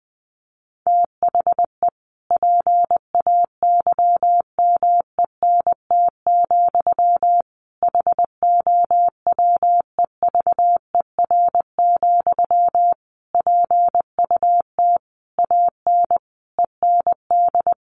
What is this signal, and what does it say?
Morse code, heard as a keyed tone, about 19 wpm, 700 Hz. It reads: THE PAYMENT, HOWEVER, PUT AN END
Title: Sentences from Top 1200 Words 20wpm